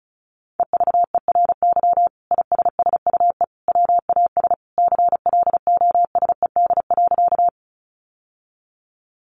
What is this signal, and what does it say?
Morse code, heard as a keyed tone, about 34 wpm, 700 Hz. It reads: EVERY ISSUE WAS CLOSED.
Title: Sentences from Top 1400 Words 35wpm